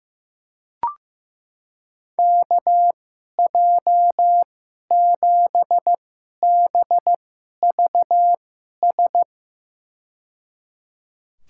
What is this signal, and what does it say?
Morse code, heard as a keyed tone, about 15 wpm, spaced as for 11 wpm, 700 Hz. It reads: KJ7BVS
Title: USA Call Signs - 6 Characters - 2x Character Spacing - 15wpm